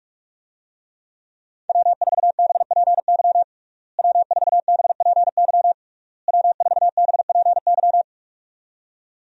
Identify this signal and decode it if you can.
Morse code, heard as a keyed tone, about 42 wpm, 700 Hz. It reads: W4BPY W4BPY W4BPY
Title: USA Call Signs - 5 Characters - Repeated 3x - 45wpm